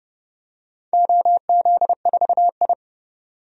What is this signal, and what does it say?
Morse code, heard as a keyed tone, about 31 wpm, 700 Hz. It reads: OZ4I
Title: DXCC-100 Call Signs - 4 Characters - 30wpm